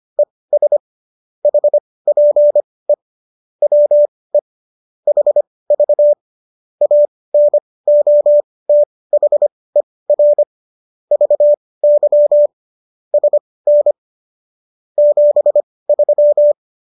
Morse code, heard as a keyed tone, about 25 wpm, spaced as for 18 wpm, 590 Hz. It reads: ES HPE WE HV ANOTHER VY SN 73
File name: qso218 18